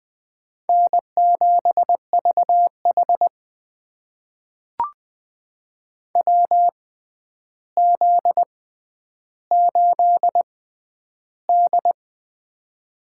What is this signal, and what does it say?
Morse code, heard as a keyed tone, about 20 wpm, spaced as for 10 wpm, 700 Hz. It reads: N7VH WZ8D
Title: USA Call Signs - 4 Characters - 6x Character Spacing - 20wpm